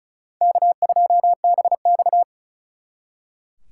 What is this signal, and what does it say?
Morse code, heard as a keyed tone, about 47 wpm, spaced as for 41 wpm, 700 Hz. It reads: K2BX
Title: Speed Racing - USA 1x2 Call Signs 35wpm